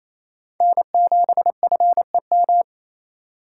Morse code, heard as a keyed tone, about 28 wpm, 700 Hz. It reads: N7FEM
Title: USA Call Signs - 4x Character Spacing - 28wpm